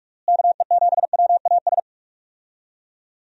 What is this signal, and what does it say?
Morse code, heard as a keyed tone, about 46 wpm, 700 Hz. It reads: KE7WAS
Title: USA Call Signs - 6 Characters - Repeated 3x - 45wpm